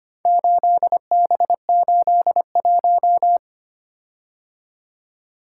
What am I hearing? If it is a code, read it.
Morse code, heard as a keyed tone, about 25 wpm, 700 Hz. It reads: 8B81
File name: practice / 25wpm / Four Letter-Number Combinations - 25wpm